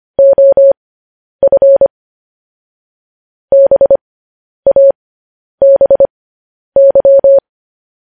Morse code, heard as a keyed tone, about 25 wpm, spaced as for 10 wpm, 550 Hz. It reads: OF BABY
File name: podcast / News headlines-1x-25x10